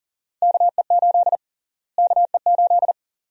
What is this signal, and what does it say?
Morse code, heard as a keyed tone, about 38 wpm, 700 Hz. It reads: KE8 KE8
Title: USA Call Sign Prefixes - Repeated 2x - 40wpm